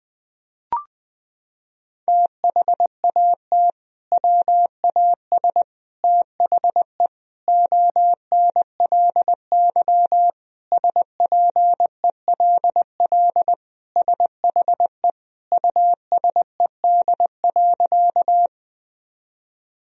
Morse code, heard as a keyed tone, about 20 wpm, 700 Hz. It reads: THAT WAS THE ONLY SPELL SHE USED.
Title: Sentences from Top 1000 Words - Review Part 1 - 20wpm